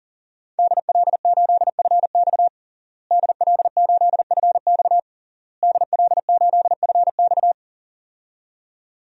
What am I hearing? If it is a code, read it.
Morse code, heard as a keyed tone, about 37 wpm, 700 Hz. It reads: DL8FX DL8FX DL8FX
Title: DXCC-100 Call Signs - 5 Characters - Repeated 3x - 40wpm